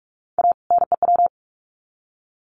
Morse code, heard as a keyed tone, about 44 wpm, 700 Hz. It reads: A NEW
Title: Sets of 2 Words 45wpm